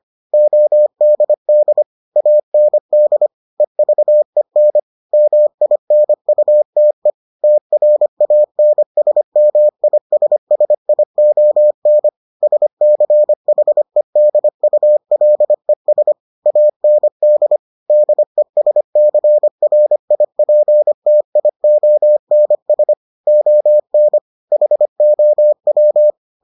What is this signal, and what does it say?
Morse code, heard as a keyed tone, about 25 wpm, 600 Hz. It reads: ODD AND EVEN MINUTE TRANSMISSION SCHEDULES AND DESCRIPTIONS ON HOW